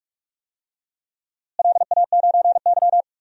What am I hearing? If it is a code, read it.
Morse code, heard as a keyed tone, about 41 wpm, 700 Hz. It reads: LA9Y
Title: DXCC-100 Call Signs - 4 Characters - Repeated 3x - 45wpm